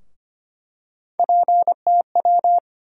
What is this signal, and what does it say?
Morse code, heard as a keyed tone, about 25 wpm, 700 Hz. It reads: PTW
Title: USA Call Sign Postfixes - 3 Letters 25wpm